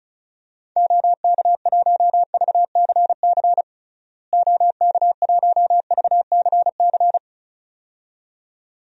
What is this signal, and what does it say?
Morse code, heard as a keyed tone, about 36 wpm, 700 Hz. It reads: OK1VCC OK1VCC
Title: DXCC-100 Call Signs - 6 Characters - Repeated 2x - 35wpm